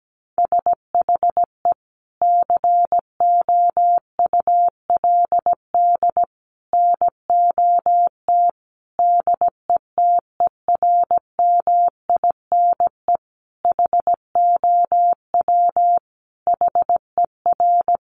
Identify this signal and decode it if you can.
Morse code, heard as a keyed tone, about 17 wpm, 700 Hz. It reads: SHE COULD NOT DETERMINE HOW HER
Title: Sentences from Top 700 Words 17wpm